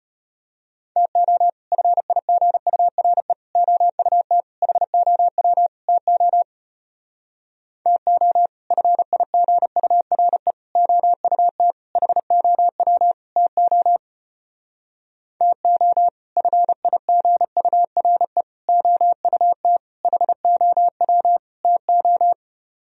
Morse code, heard as a keyed tone, about 33 wpm, 700 Hz. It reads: TO FIGURE OUT HOW TO TO FIGURE OUT HOW TO TO FIGURE OUT HOW TO